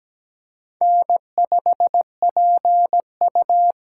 Morse code, heard as a keyed tone, about 17 wpm, 700 Hz. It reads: N5PU
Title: K1USN SST QSO Practice - 17wpm